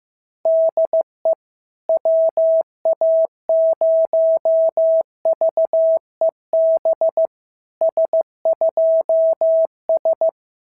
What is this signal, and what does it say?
Morse code, heard as a keyed tone, about 15 wpm, 650 Hz. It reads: DE WA0VEB S2S
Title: SOTA QSO Practice 15wpm